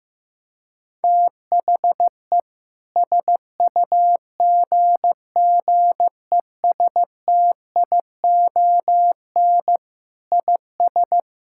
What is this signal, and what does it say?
Morse code, heard as a keyed tone, about 15 wpm, 700 Hz. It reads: THE SUGGESTION IS
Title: Sentences from Top 1600 Words 15wpm